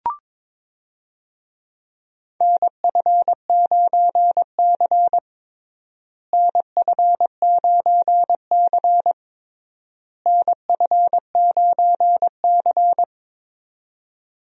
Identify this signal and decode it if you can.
Morse code, heard as a keyed tone, about 22 wpm, 700 Hz. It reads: NF9C NF9C NF9C
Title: USA Call Signs - 4 Characters - Repeated 3x - 22wpm